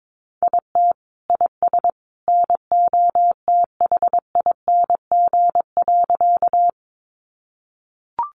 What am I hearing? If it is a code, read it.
Morse code, heard as a keyed tone, about 22 wpm, 700 Hz. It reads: IT IS NOTHING.
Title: Sentences from Top 500 Words 22wpm